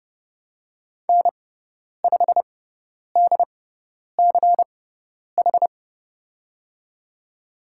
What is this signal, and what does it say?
Morse code, heard as a keyed tone, about 31 wpm, spaced as for 10 wpm, 700 Hz. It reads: N5DCH
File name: Lesson 17g - Letter u - Call Signs 30wpm